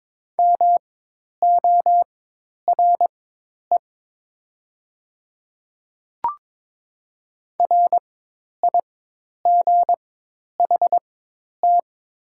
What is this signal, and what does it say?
Morse code, heard as a keyed tone, about 22 wpm, spaced as for 10 wpm, 700 Hz. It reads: MORE RIGHT
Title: 4x Character Spacing - Top 300 Words - 22wpm